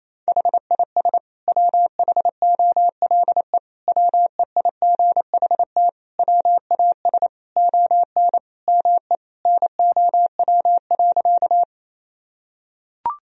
Morse code, heard as a keyed tone, about 28 wpm, 700 Hz. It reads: HIS WHOLE WEIGHT WAS ON ME NOW.
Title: Sentences from Top 1000 Words - Review Part 3 - 28wpm